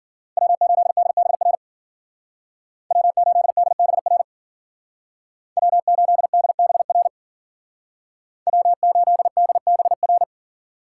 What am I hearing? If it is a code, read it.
Morse code, heard as a keyed tone, about 48 wpm, 700 Hz. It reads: W8DBR W8DBR W8DBR W8DBR
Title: Speed Racing - USA 1x3 Call Signs 40wpm